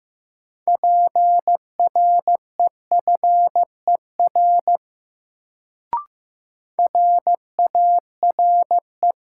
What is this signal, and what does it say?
Morse code, heard as a keyed tone, about 15 wpm, 700 Hz. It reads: PREFER RARE
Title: Sentences from Top 1100 Words - New Word Drill Alt - 15wpm